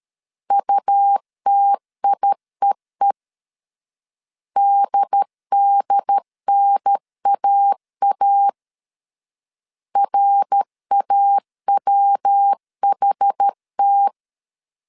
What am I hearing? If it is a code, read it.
Morse code, heard as a keyed tone, about 13 wpm, 800 Hz. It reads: UTIEE DDNAA RAWHT
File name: CW / DET / 1 3 1500C S C 12wpm 2750